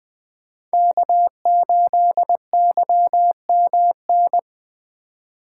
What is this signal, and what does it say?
Morse code, heard as a keyed tone, about 20 wpm, 700 Hz. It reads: K8YMN
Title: Speed Racing - USA 1x3 Call Signs 17wpm